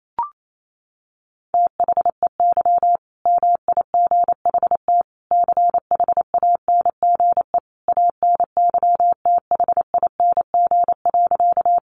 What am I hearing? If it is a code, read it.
Morse code, heard as a keyed tone, about 28 wpm, 700 Hz. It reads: THEY MIGHT CHANGE ANYTHING.
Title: A Sound Of Thunder 28wpm